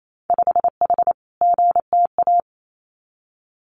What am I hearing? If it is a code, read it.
Morse code, heard as a keyed tone, about 28 wpm, 700 Hz. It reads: 5H GTA
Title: Winter Field Day Exchanges Encore 28wpm